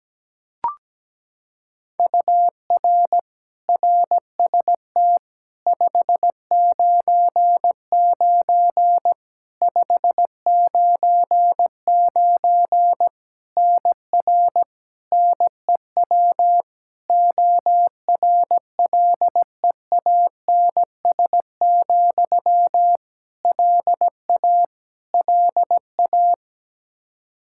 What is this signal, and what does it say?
Morse code, heard as a keyed tone, about 17 wpm, 700 Hz. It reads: UR RST 599 599 NR NEW ORLEANS, LA LA
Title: Practice QSO - Exchange 1 - Beginning - 17wpm